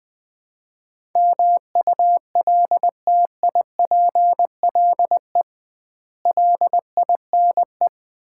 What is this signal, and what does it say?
Morse code, heard as a keyed tone, about 20 wpm, 700 Hz. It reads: MULTIPLE LINE
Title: 2x Word-Spacing - Sets of 5 Words - 20wpm